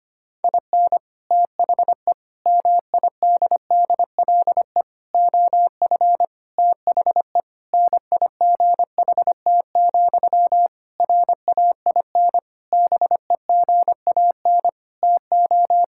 Morse code, heard as a keyed tone, about 25 wpm, 700 Hz. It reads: IN THE MIDDLE OF THE NIGHT, RAIN BEGAN TO
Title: Sentences from Top 1000 Words - Review Part 6 - 25wpm